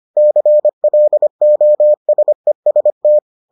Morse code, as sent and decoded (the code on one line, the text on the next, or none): -.-. .-.. --- ... ... -
CLOSEST